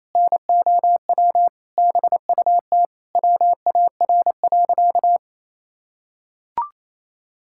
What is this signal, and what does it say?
Morse code, heard as a keyed tone, about 28 wpm, 700 Hz. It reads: NOW BUT WAR.
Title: Sentences from Top 1000 Words - Review Part 6 - 28wpm